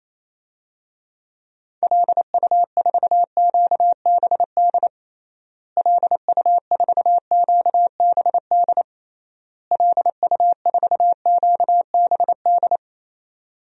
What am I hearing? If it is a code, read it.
Morse code, heard as a keyed tone, about 28 wpm, 700 Hz. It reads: LU4QBD LU4QBD LU4QBD
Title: DXCC-100 Call Signs - 6 Characters - Repeated 3x - 28wpm